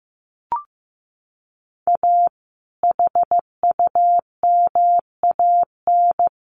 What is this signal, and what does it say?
Morse code, heard as a keyed tone, about 15 wpm, 700 Hz. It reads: A HUMAN
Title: Sentences from Top 1200 Words 15wpm